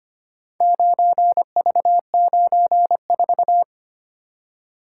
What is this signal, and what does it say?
Morse code, heard as a keyed tone, about 25 wpm, 700 Hz. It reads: 9V94
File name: practice / 25wpm / Four Letter-Number Combinations - 25wpm